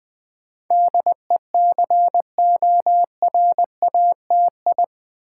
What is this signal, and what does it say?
Morse code, heard as a keyed tone, about 20 wpm, 700 Hz. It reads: DECORATI
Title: Build A Word - 4 Letters At A Time - 10-Letter Words - 20wpm